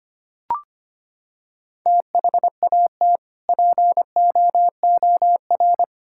Morse code, heard as a keyed tone, about 25 wpm, 700 Hz. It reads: THAT POOR
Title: Sentences from Top 1000 Words - Review Part 5 - 25wpm